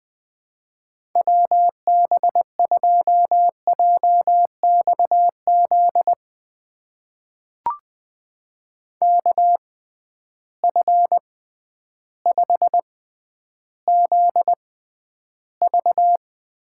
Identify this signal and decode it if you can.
Morse code, heard as a keyed tone, about 21 wpm, spaced as for 11 wpm, 700 Hz. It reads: WB2JXZ KF5ZV
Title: USA Call Signs - 6x Character Spacing - 20wpm